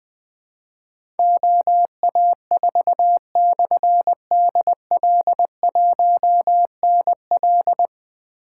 Morse code, heard as a keyed tone, about 20 wpm, 700 Hz. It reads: OA4/DL1NL
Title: Difficult Call Signs 20wpm